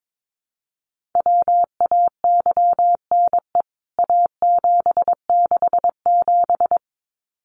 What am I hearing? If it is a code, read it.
Morse code, heard as a keyed tone, about 22 wpm, 700 Hz. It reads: WAYNE A767
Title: CWT Exchange 22wpm